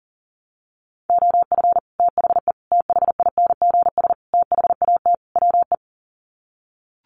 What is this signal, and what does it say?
Morse code, heard as a keyed tone, about 40 wpm, 700 Hz. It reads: OF THE THINGS THAT WE
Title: Sets of 5 Words Encore 40wpm